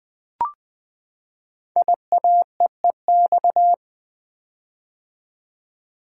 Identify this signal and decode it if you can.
Morse code, heard as a keyed tone, about 20 wpm, 700 Hz. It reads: IAEEX
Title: Code Groups - 5 Letter:Numbers - 20wpm